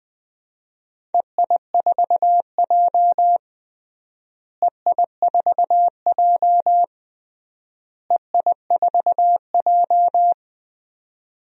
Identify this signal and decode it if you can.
Morse code, heard as a keyed tone, about 20 wpm, 700 Hz. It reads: EI4J EI4J EI4J
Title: DXCC-100 Call Signs - 4 Characters - Repeated 3x - 20wpm